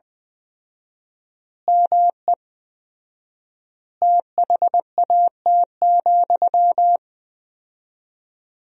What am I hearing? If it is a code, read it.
Morse code, heard as a keyed tone, about 20 wpm, 700 Hz. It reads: ME THAT,
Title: 4x Word-Spacing - Sentences from Top 500 Words - 20wpm